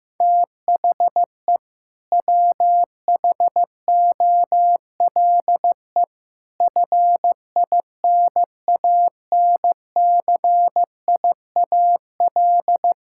Morse code, heard as a keyed tone, about 15 wpm, 700 Hz. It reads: THE WHOLE FINANCIAL
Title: Sentences from Top 1000 Words - Review Part 5 - 15wpm